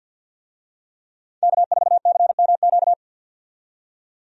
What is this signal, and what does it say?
Morse code, heard as a keyed tone, about 52 wpm, 700 Hz. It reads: K4CMQ
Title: USA Call Signs - 5 Characters - 50wpm